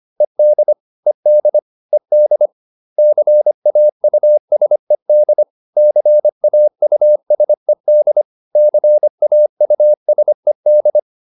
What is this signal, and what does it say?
Morse code, heard as a keyed tone, about 25 wpm, 600 Hz. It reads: ED ED ED CAUSED CAUSED CAUSED